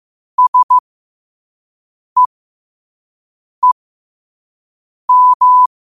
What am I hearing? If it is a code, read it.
Morse code, heard as a keyed tone, about 15 wpm, spaced as for 5 wpm, 1000 Hz. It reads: SEEM
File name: qotdfiles / 5wpm